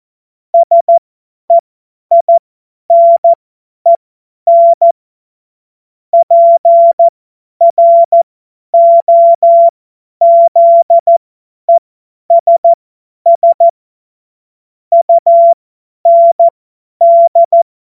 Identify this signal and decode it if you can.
Morse code, heard as a keyed tone, about 14 wpm, spaced as for 10 wpm, 680 Hz. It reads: SEINEN PROZESS UND
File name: Morsekurs / Witze / Witz 09. August 2022 50 F70